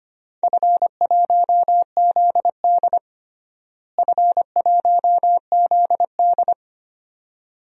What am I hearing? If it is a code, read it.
Morse code, heard as a keyed tone, about 25 wpm, 700 Hz. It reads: F1ZD F1ZD
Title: DXCC-100 Call Signs - 4 Characters - Repeated 2x - 25wpm